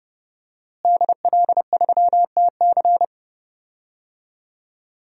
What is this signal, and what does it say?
Morse code, heard as a keyed tone, about 30 wpm, 700 Hz. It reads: DL3TC
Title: DXCC-100 Call Signs - 5 Characters - Repeated 3x - 30wpm